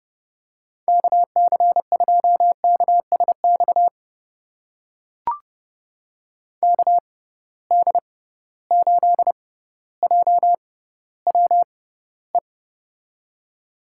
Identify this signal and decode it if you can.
Morse code, heard as a keyed tone, about 32 wpm, spaced as for 16 wpm, 700 Hz. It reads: KC2KSX KD8JWE
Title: USA Call Signs - 6 Characters - 6x Character Spacing - 30wpm